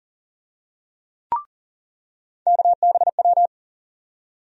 Morse code, heard as a keyed tone, about 40 wpm, 700 Hz. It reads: KBW
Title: USA Call Sign Postfixes - 3 Letters 40wpm